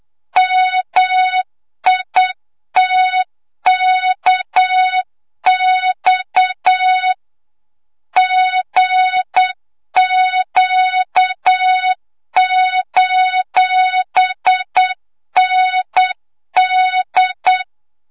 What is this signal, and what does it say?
Morse code, heard as a keyed tone, about 8 wpm, 3700 Hz, with noise. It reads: MITKX GQ:ND
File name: Morze-HA2MN / hangp / 2 06 GYAKORLAT